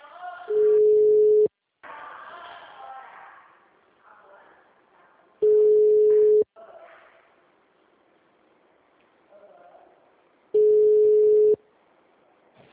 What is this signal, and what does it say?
Ringback tone (Europe, 425 Hz), with speech.
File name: out- -2015